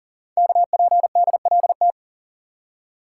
Morse code, heard as a keyed tone, about 42 wpm, 690 Hz. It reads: KPDLT
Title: Code Groups - 5 Letters - 40wpm